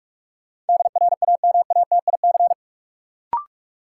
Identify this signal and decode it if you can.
Morse code, heard as a keyed tone, about 46 wpm, 700 Hz. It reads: DRAMATIC
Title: Sentences from Top 1300 Words - New Word Drill Alt - 45wpm